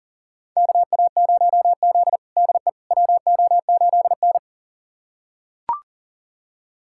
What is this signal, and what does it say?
Morse code, heard as a keyed tone, about 40 wpm, 700 Hz. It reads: KA0Z DE WO8N
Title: USA Callsign DE Callsign - 4 Characters - 40wpm